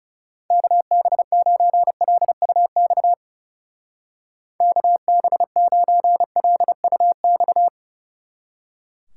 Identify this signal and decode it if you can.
Morse code, heard as a keyed tone, about 36 wpm, 700 Hz. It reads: KB9LUX KB9LUX
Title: Speed Racing - USA 2x3 Call Signs 30wpm